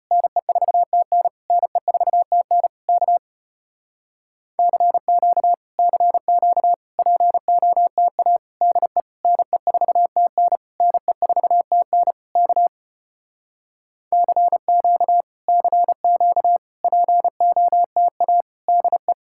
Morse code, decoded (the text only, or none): NE4TN NE4TN K CQ CQ POTA DE NE4TN NE4TN K CQ CQ POTA DE